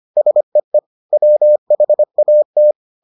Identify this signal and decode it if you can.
Morse code, heard as a keyed tone, about 25 wpm, 600 Hz. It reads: SEE WHAT